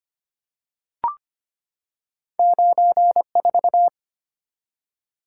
Morse code, heard as a keyed tone, about 25 wpm, 700 Hz. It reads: 94
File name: Two Letter-Number Combinations - 25wpm